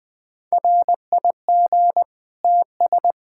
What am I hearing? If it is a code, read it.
Morse code, heard as a keyed tone, about 21 wpm, 700 Hz. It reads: RIG TS
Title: Practice QSO - Exchange 2 - Info - 20wpm